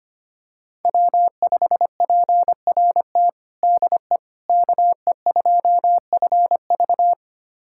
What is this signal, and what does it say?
Morse code, heard as a keyed tone, about 25 wpm, 700 Hz. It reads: W5PRT DE KE2FV
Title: USA Callsign DE Callsign - 5 Characters - 25wpm